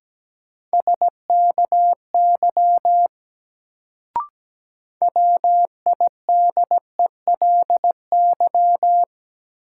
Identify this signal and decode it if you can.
Morse code, heard as a keyed tone, about 17 wpm, 700 Hz. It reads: SKY WIDELY
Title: Sentences from Top 1200 Words - New Word Drill Alt - 17wpm